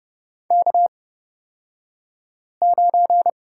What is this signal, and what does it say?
Morse code, heard as a keyed tone, about 30 wpm, spaced as for 5 wpm, 700 Hz. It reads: K9
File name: Lesson 37d - Letter x - Two Characters 30wpm